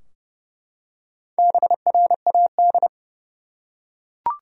Morse code, heard as a keyed tone, about 30 wpm, 700 Hz. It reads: BRAD
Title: USA Common Names - Max 4 Letters - 30wpm